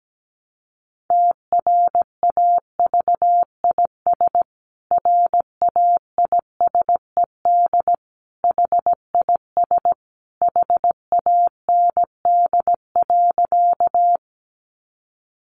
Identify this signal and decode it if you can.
Morse code, heard as a keyed tone, about 17 wpm, 700 Hz. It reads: TRAVIS RAISED HIS HAND.
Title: A Sound Of Thunder 17wpm